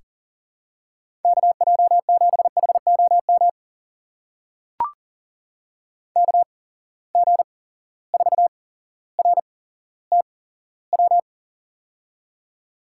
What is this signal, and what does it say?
Morse code, heard as a keyed tone, about 40 wpm, spaced as for 17 wpm, 700 Hz. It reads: KJ7HOM KG4RTW
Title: USA Call Signs - 6 Characters - 8x Character Spacing - 40wpm